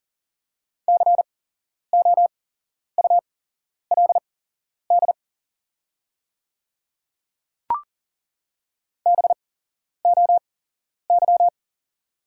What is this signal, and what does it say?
Morse code, heard as a keyed tone, about 40 wpm, spaced as for 11 wpm, 690 Hz. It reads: COULD BOY
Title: 8x Character Spacing - Top 300 Words - 40wpm